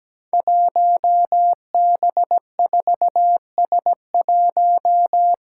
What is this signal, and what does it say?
Morse code, heard as a keyed tone, about 17 wpm, 700 Hz. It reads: 1B4S1
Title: Five Challenging Letter-Number Combinations 17wpm